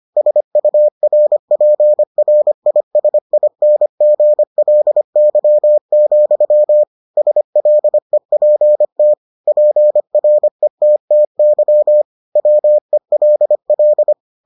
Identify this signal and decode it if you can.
Morse code, heard as a keyed tone, about 25 wpm, 600 Hz. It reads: SURPRISINGLY, SLEPT PRETTY WELL